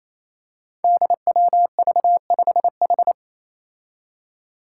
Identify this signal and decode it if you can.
Morse code, heard as a keyed tone, about 28 wpm, 700 Hz. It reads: DWV5H
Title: Five Challenging Letter-Number Combinations 28wpm